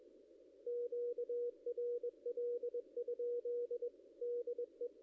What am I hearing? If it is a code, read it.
Morse code, heard as a keyed tone, about 21 wpm, 480 Hz. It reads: QRL? DE
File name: contests / ss-cw-k5tr-op-n5rz -1700